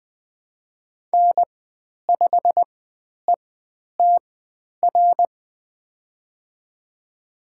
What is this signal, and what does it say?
Morse code, heard as a keyed tone, about 20 wpm, spaced as for 10 wpm, 700 Hz. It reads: N5ETR
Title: 095 Lesson 16g - Letter c - Call Signs 20wpm